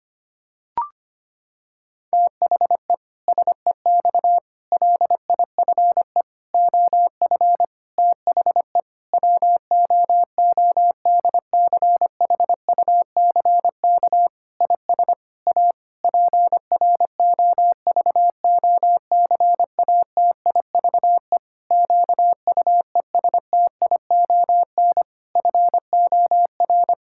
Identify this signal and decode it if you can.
Morse code, heard as a keyed tone, about 25 wpm, 700 Hz. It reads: THE SEX LIFE OF THE WOODCHUCK IS A PROVOCATIVE QUESTION FOR
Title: Pangram Challenge v2 - 25wpm